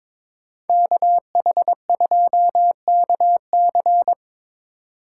Morse code, heard as a keyed tone, about 22 wpm, 700 Hz. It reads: KH2KC
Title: USA Call Signs - 2x Character Spacing - 22wpm